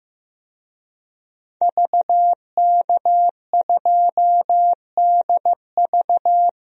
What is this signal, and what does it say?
Morse code, heard as a keyed tone, about 15 wpm, 700 Hz. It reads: VK2DV
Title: DXCC-100 Call Signs - 5 Characters - Repeated 2x - 15wpm